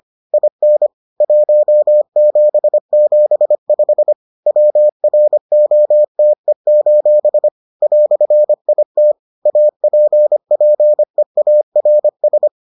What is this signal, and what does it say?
Morse code, heard as a keyed tone, about 25 wpm, 600 Hz. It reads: IN 1775 WROTE: "IT APPEARS